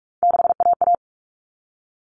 Morse code, heard as a keyed tone, about 44 wpm, 700 Hz. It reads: 6AA
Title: All Cut Numbers - 3-Digit 45wpm